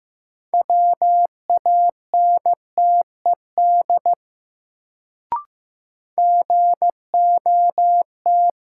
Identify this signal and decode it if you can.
Morse code, heard as a keyed tone, about 15 wpm, 700 Hz. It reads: WANTED GOT
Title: Sets of 2 Words - New Word Drill - 15wpm